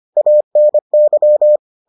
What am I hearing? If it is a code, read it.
Morse code, heard as a keyed tone, about 25 wpm, 600 Hz. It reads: ANY